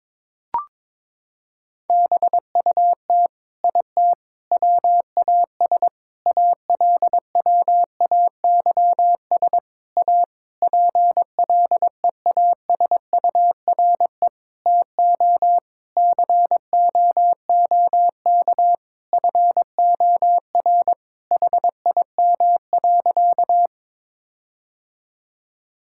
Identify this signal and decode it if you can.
Morse code, heard as a keyed tone, about 22 wpm, 700 Hz. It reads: BUT IT WAS ALWAYS A PLEASURE TO COOK FOR HIM.
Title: Sentences from Top 1600 Words 22wpm